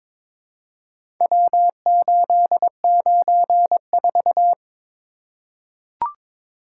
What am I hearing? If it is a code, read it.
Morse code, heard as a keyed tone, about 22 wpm, 700 Hz. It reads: W894
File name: practice / 22wpm / Four Letter-Number Combinations - 22wpm